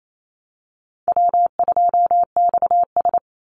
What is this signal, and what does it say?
Morse code, heard as a keyed tone, about 28 wpm, 700 Hz. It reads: W2XS
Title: Easy Call Signs Repeated 2x - 28wpm